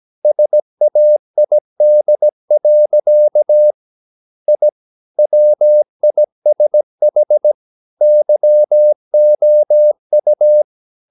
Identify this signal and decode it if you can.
Morse code, heard as a keyed tone, about 17 wpm, 600 Hz. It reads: SAID. I WISH YOU
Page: Romanen Never af Ken Follet på engelsk.